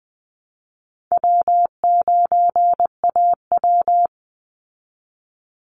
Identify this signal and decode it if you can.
Morse code, heard as a keyed tone, about 20 wpm, 700 Hz. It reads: W9AW
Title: Easy Call Signs Repeated 3x - 20wpm